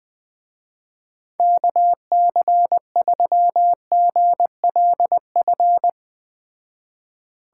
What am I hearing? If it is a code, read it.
Morse code, heard as a keyed tone, about 20 wpm, 700 Hz. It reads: KC3GLF
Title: Speed Racing - USA Call Signs 17wpm